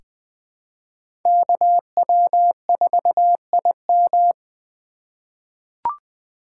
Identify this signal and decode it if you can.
Morse code, heard as a keyed tone, about 20 wpm, 700 Hz. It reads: KW4IM
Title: USA Call Signs - 5 Characters - 2x Character Spacing - 20wpm